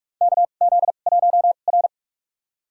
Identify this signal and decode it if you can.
Morse code, heard as a keyed tone, about 46 wpm, spaced as for 32 wpm, 700 Hz. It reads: KZ1R
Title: USA Call Signs - 2x Character Spacing - 45wpm